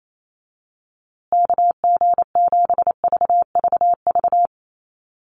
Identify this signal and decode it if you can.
Morse code, heard as a keyed tone, about 28 wpm, 700 Hz. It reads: KG7VVV
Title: USA Call Signs Repeated 3x 28wpm